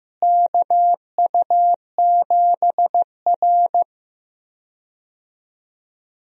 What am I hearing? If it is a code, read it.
Morse code, heard as a keyed tone, about 16 wpm, 700 Hz. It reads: KU7R
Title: USA Call Signs - 4 Characters - Repeated 2x - 15wpm